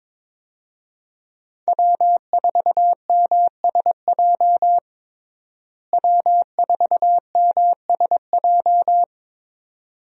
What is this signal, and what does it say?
Morse code, heard as a keyed tone, about 22 wpm, 700 Hz. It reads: W4MSJ W4MSJ
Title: USA Call Signs - 5 Characters - Repeated 2x - 22wpm